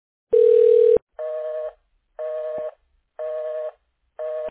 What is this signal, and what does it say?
Busy tone (North America), 480 + 620 Hz.